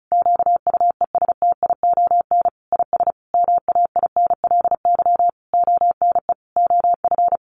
Morse code, heard as a keyed tone, about 34 wpm, 700 Hz. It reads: QUESTION IS MAINLY ONE OF
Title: Sentences from Top 1400 Words 35wpm